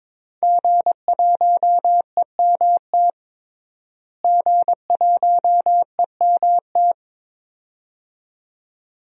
Morse code, heard as a keyed tone, about 22 wpm, 700 Hz. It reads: G1EMT G1EMT
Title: DXCC-100 Call Signs - 5 Characters - Repeated 2x - 22wpm